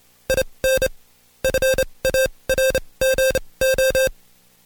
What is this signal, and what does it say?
Morse code, heard as a keyed tone, about 27 wpm, spaced as for 21 wpm, 530 Hz. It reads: IN FARGO
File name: Code Quick / AudioDownld2 / Seg137-Stepper10 21WPM